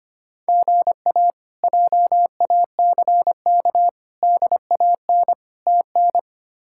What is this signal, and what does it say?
Morse code, heard as a keyed tone, about 25 wpm, 700 Hz. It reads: GA JACK DAN TN
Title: K1USN SST QSO Practice - 25wpm